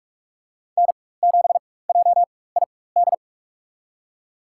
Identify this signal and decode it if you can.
Morse code, heard as a keyed tone, about 42 wpm, spaced as for 20 wpm, 700 Hz. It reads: N7JID
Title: USA Call Signs - 5 Characters - 4x Character Spacing - 45wpm